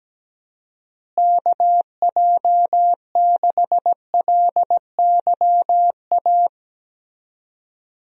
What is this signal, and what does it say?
Morse code, heard as a keyed tone, about 17 wpm, 700 Hz. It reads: KJ6LYA
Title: USA Call Signs - 6 Characters - Repeated 2x - 17wpm